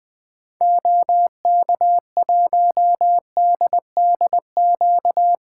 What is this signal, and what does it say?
Morse code, heard as a keyed tone, about 21 wpm, 700 Hz. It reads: OK1DDQ
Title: 2023 SOTA Activator Call Signs - 20wpm